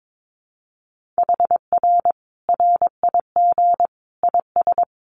Morse code, heard as a keyed tone, about 22 wpm, 700 Hz. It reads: HR RIG IS
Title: Common QSO Phrases 22wpm